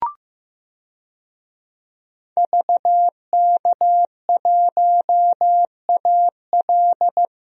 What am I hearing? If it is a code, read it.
Morse code, heard as a keyed tone, about 15 wpm, 700 Hz. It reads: VK1AL
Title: DXCC-100 Call Signs - 5 Characters - Repeated 3x - 15wpm